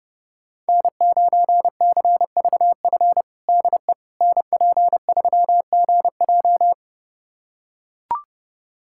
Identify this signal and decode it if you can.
Morse code, heard as a keyed tone, about 32 wpm, 700 Hz. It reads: N9CVF DE NP3GJ
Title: USA Callsign DE Callsign - 5 Characters - 30wpm